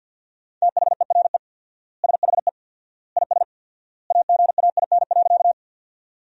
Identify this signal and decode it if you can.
Morse code, heard as a keyed tone, about 47 wpm, 700 Hz. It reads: THERE SHE IS AGAIN.
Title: 4x Word-Spacing - Sentences from Top 200 Words - 50wpm